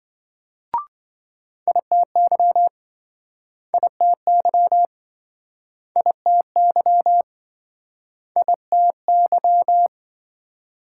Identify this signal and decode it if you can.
Morse code, heard as a keyed tone, about 27 wpm, 700 Hz. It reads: ITY ITY ITY ITY